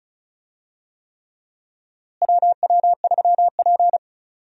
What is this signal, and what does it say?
Morse code, heard as a keyed tone, about 34 wpm, 700 Hz. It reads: WW3P
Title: USA Call Signs - 4 Characters - Repeated 3x - 35wpm